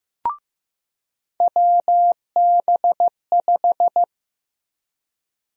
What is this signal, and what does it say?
Morse code, heard as a keyed tone, about 16 wpm, 700 Hz. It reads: WB5
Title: USA Call Sign Prefixes - 15wpm